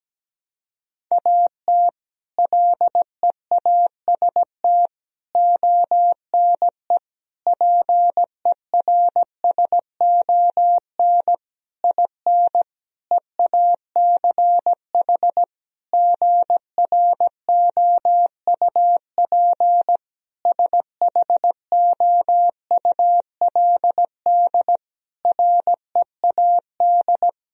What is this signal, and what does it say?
Morse code, heard as a keyed tone, about 17 wpm, 700 Hz. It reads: AT LEAST ONE PERSON IN EACH GROUP SHOULD READ
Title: Sentences from Top 1000 Words - Review Part 1 - 17wpm